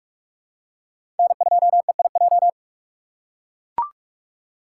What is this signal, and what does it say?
Morse code, heard as a keyed tone, about 42 wpm, 700 Hz. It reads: N2EIJ
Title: USA Call Signs - 5 Characters - 2x Character Spacing - 45wpm